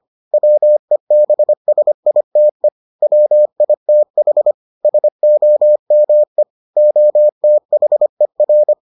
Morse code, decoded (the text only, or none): WEBSITE WITH SOME OTHER